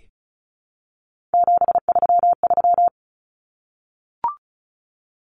Morse code, heard as a keyed tone, about 34 wpm, 700 Hz. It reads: 733
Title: Common Cut Numbers - 3-Digit 35wpm